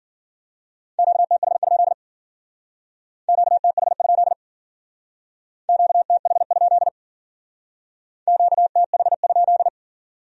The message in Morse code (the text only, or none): QTH? QTH? QTH? QTH?